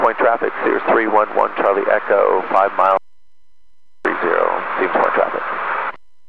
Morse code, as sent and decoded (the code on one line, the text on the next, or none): none